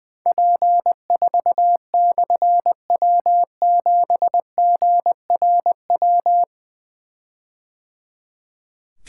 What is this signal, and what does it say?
Morse code, heard as a keyed tone, about 20 wpm, 700 Hz. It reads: P4/W7GRW
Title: Difficult Call Signs 20wpm